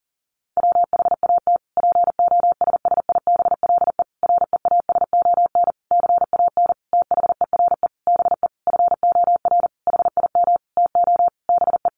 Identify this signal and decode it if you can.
Morse code, heard as a keyed tone, about 40 wpm, 700 Hz. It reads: WHAT POSSIBLE REASON CAN THERE BE FOR HIM TO BE
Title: Sentences from Top 1300 Words 40wpm